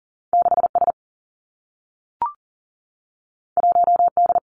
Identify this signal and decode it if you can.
Morse code, heard as a keyed tone, about 40 wpm, 700 Hz. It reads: 6 S 1 D
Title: Two Challenging Letter-Number Combinations 40wpm